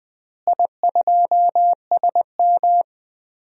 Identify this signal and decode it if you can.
Morse code, heard as a keyed tone, about 20 wpm, 700 Hz. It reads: I2SM
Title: DXCC-100 Call Signs - 4 Characters - Repeated 3x - 20wpm